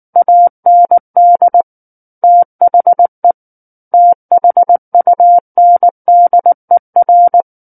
Morse code, heard as a keyed tone, about 19 wpm, 700 Hz. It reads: AND THE THUNDER
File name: Book One / 10-Chapter 10 - 19 WPM